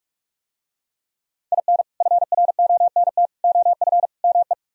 Morse code, heard as a keyed tone, about 44 wpm, 700 Hz. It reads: IN FRONT OF ME